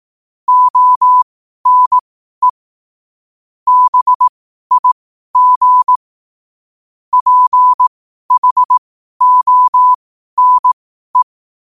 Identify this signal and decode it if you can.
Morse code, heard as a keyed tone, about 18 wpm, spaced as for 12 wpm, 1000 Hz. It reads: ONE BIG PHONE
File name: qotdfiles / 12wpm